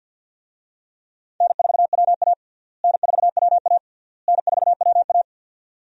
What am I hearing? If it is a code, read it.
Morse code, heard as a keyed tone, about 52 wpm, 700 Hz. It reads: N4WA N4WA N4WA
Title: USA Call Signs - 4 Characters - Repeated 3x - 50wpm